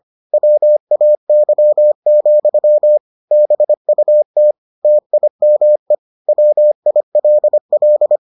Morse code, heard as a keyed tone, about 25 wpm, 600 Hz. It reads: WAY, BUT TIME WILL